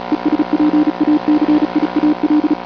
Morse code, heard as a keyed tone, about 36 wpm, 300 Hz. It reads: ESPACIAL